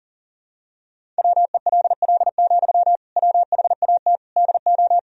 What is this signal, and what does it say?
Morse code, heard as a keyed tone, about 40 wpm, 700 Hz. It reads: WELL, WHAT DO
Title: Sentences from Top 200 Words and Abbreviations 40wpm